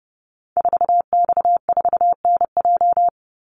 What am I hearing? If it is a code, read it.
Morse code, heard as a keyed tone, about 28 wpm, 700 Hz. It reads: 4X4NJ
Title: Call Signs - 30wpm